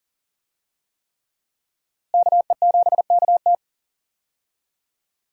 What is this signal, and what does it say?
Morse code, heard as a keyed tone, about 40 wpm, 700 Hz. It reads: KE7KT
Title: USA Call Signs - 5 Characters - Repeated 3x - 40wpm